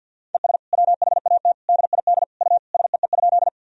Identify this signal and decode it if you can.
Morse code, heard as a keyed tone, about 47 wpm, 700 Hz. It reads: ES WHAT DID U SEE?
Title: Sentences from Top 100 Words and Abbreviations 50wpm